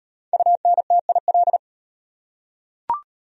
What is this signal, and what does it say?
Morse code, heard as a keyed tone, about 35 wpm, 700 Hz. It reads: UNTIL